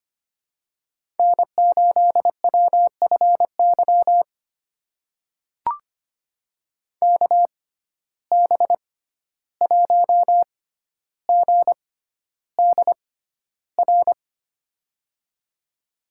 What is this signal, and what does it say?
Morse code, heard as a keyed tone, about 25 wpm, spaced as for 12 wpm, 700 Hz. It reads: N8WFY KB1GDR
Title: USA Call Signs - 6x Character Spacing - 25wpm